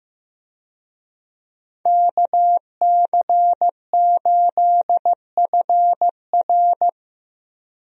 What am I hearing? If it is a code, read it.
Morse code, heard as a keyed tone, about 15 wpm, 700 Hz. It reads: KC8FR
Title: USA Call Signs - 5 Characters - Repeated 2x - 15wpm